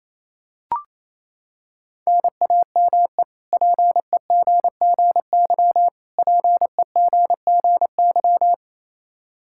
Morse code, heard as a keyed tone, about 28 wpm, 700 Hz. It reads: NAME PEGGY PEGGY
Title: K1USN SST QSO Practice 28wpm